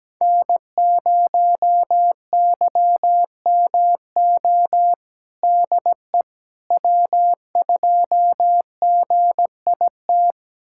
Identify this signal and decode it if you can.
Morse code, heard as a keyed tone, about 17 wpm, 700 Hz. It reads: N0YMO DE W2GIT
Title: USA Callsign DE Callsign - 5 Characters - 17wpm